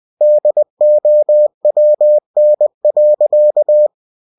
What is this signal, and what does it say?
Morse code, heard as a keyed tone, about 20 wpm, 600 Hz. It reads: DOWN.
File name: cw-books / winnie-the-pooh / winnie-20wpm-page-0038